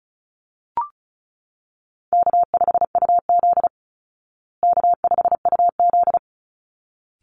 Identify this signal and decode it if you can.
Morse code, heard as a keyed tone, about 34 wpm, 700 Hz. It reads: K5UZ K5UZ
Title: Easy Call Signs Repeated 2x - 35wpm